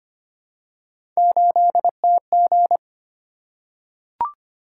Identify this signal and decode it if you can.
Morse code, heard as a keyed tone, about 25 wpm, 700 Hz. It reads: 8TG
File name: Three Letter-Number Combinations - 25wpm